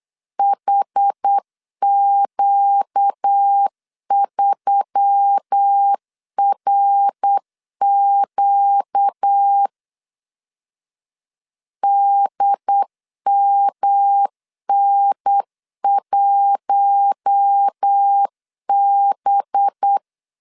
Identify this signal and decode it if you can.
Morse code, heard as a keyed tone, about 8 wpm, 800 Hz. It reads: HQ3RQ DMN1B
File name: 8wpm